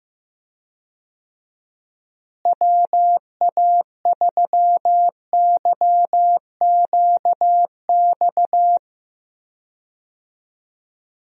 Morse code, heard as a keyed tone, about 15 wpm, 700 Hz. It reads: WA3YQX
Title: USA Call Signs - 6 Characters - Repeated 2x - 15wpm